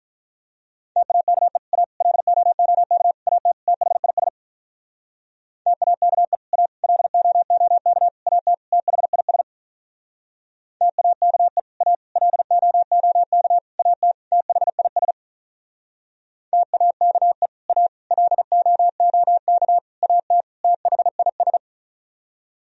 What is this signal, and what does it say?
Morse code, heard as a keyed tone, about 46 wpm, 700 Hz. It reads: TAKE A LOOK AT THIS TAKE A LOOK AT THIS TAKE A LOOK AT THIS TAKE A LOOK AT THIS